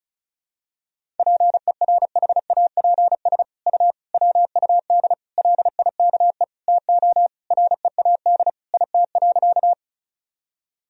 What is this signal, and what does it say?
Morse code, heard as a keyed tone, about 36 wpm, 700 Hz. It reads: PERHAPS U WUD LIKE TO READ IT.
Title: Sentences from Top 500 Words and Abbreviations 35wpm